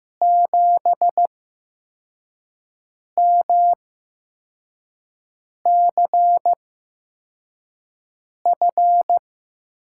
Morse code, heard as a keyed tone, about 15 wpm, spaced as for 4 wpm, 700 Hz. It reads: 7MCF
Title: USA Call Signs - 6 Characters - 8x Character Spacing - 15wpm